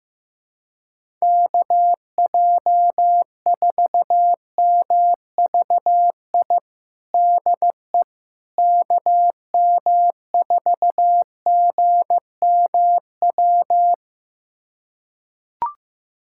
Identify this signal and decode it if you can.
Morse code, heard as a keyed tone, about 16 wpm, 700 Hz. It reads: KJ4MVI DE KM4GMW
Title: USA Callsign DE Callsign - 6 Characters - 15wpm